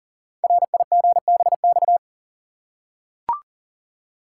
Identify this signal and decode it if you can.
Morse code, heard as a keyed tone, about 40 wpm, 690 Hz. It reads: RIGBX
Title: Code Groups - 5 Letters - 40wpm